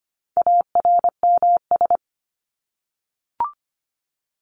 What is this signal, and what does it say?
Morse code, heard as a keyed tone, about 25 wpm, 700 Hz. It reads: ARMS
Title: ES:S Suffix - Top 100 Words with Base Words 25wpm